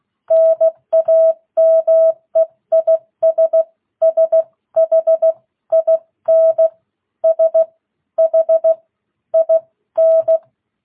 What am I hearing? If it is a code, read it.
Morse code, heard as a keyed tone, about 16 wpm, spaced as for 13 wpm, 650 Hz. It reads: NAME IS SHIN SHIN